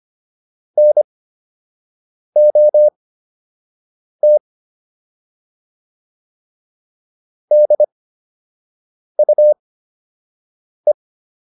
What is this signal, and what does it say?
Morse code, heard as a keyed tone, about 25 wpm, spaced as for 6 wpm, 600 Hz. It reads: NOT DUE